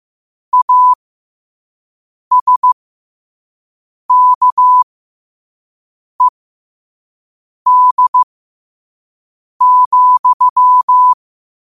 Morse code, heard as a keyed tone, about 15 wpm, spaced as for 5 wpm, 1000 Hz. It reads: ASKED,
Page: Quotes for Sat, 16 Aug 2025 in Morse Code at 5 words per minute.